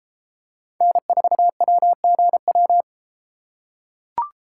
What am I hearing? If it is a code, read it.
Morse code, heard as a keyed tone, about 30 wpm, 700 Hz. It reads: N4WGW